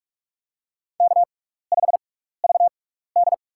Morse code, heard as a keyed tone, about 46 wpm, spaced as for 16 wpm, 700 Hz. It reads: K5VD
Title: USA Call Signs - 4 Characters - 6x Character Spacing - 45wpm